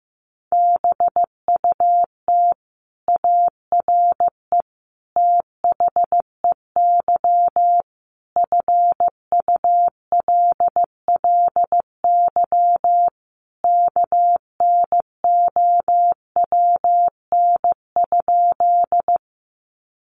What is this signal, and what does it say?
Morse code, heard as a keyed tone, about 15 wpm, 700 Hz. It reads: BUT ARE THEY FULLY KNOWN?
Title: Sentences from Top 1300 Words 15wpm